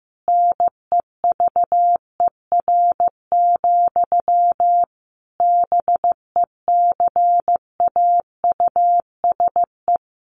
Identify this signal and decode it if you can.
Morse code, heard as a keyed tone, about 15 wpm, 700 Hz. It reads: NEVER, BECAUSE
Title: Sentences from Top 1200 Words 15wpm